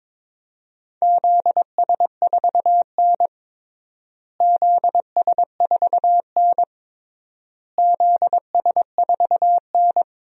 Morse code, heard as a keyed tone, about 22 wpm, 700 Hz. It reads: ZS4N ZS4N ZS4N
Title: DXCC-100 Call Signs - 4 Characters - Repeated 3x - 22wpm